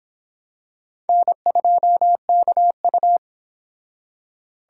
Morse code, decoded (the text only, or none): N2KU